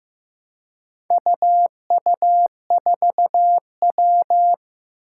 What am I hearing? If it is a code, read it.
Morse code, heard as a keyed tone, about 15 wpm, 700 Hz. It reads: UU4W
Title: Four Challenging Letter-Number Combinations 15wpm